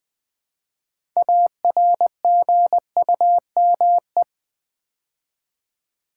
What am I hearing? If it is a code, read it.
Morse code, heard as a keyed tone, about 20 wpm, 700 Hz. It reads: ARGUME
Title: Build A Word - 3 Letters At A Time - 8-Letter Words - 20wpm